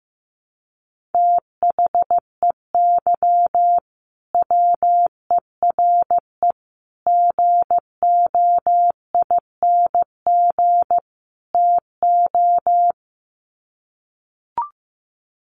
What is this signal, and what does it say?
Morse code, heard as a keyed tone, about 15 wpm, 700 Hz. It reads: THEY WERE GOING TO
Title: Sets of 4 Words 15wpm